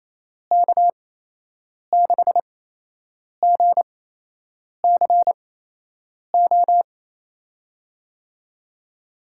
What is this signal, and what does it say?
Morse code, heard as a keyed tone, about 28 wpm, spaced as for 8 wpm, 700 Hz. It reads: K6GCO
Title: USA Call Signs - 8x Character Spacing - 28wpm